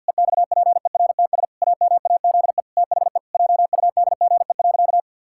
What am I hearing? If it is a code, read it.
Morse code, heard as a keyed tone, about 50 wpm, 700 Hz. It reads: EXPERTS AMAZE THE JUDGE.
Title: Pangram Challenge v2 - 50wpm